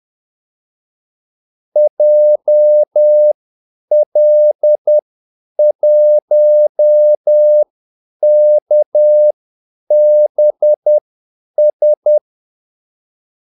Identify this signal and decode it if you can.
Morse code, heard as a keyed tone, about 10 wpm, spaced as for 8 wpm, 600 Hz. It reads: JL1KBS
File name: Callsigns-10-8wpm